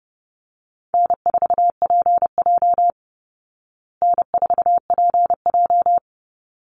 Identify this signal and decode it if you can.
Morse code, heard as a keyed tone, about 30 wpm, 700 Hz. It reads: N4PJ N4PJ
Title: Easy Call Signs Repeated 2x - 30wpm